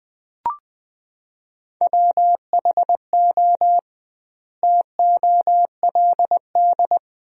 Morse code, heard as a keyed tone, about 20 wpm, 700 Hz. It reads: WHO TOLD
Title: 2x Word-Spacing - Sentences from Top 400 Words - 20wpm